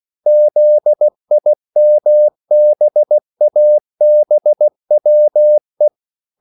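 Morse code, heard as a keyed tone, about 16 wpm, 600 Hz. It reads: ZIMBABWE
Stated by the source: CW practice of national capitals at 16 words per minute, from ON6ZQ.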